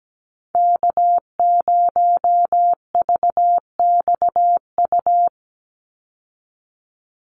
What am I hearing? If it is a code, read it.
Morse code, heard as a keyed tone, about 17 wpm, 700 Hz. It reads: K0VXU
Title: Call Signs Repeated 3x - 17wpm